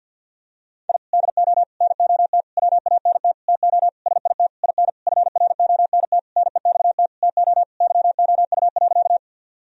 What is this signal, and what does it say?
Morse code, heard as a keyed tone, about 50 wpm, 700 Hz. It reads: I DO NOT WANT TO SIT IN FRONT NEXT TO YOU.
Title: Sentences from Top 1000 Words - Review Part 2 - 50wpm